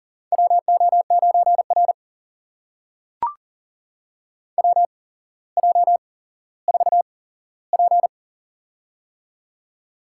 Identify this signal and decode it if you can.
Morse code, heard as a keyed tone, about 38 wpm, spaced as for 17 wpm, 700 Hz. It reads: WO9R WJ4P
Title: USA Call Signs - 4 Characters - 8x Character Spacing - 40wpm